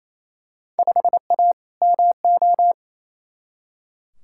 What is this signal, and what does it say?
Morse code, heard as a keyed tone, about 28 wpm, 700 Hz. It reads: 5A MO
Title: Field Day Exchanges Encore - 28wpm